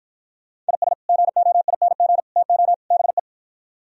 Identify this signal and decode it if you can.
Morse code, heard as a keyed tone, about 53 wpm, 700 Hz. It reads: IS GOING TO BE